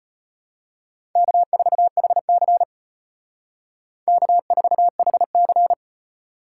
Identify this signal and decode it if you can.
Morse code, heard as a keyed tone, about 32 wpm, 700 Hz. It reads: K4HC K4HC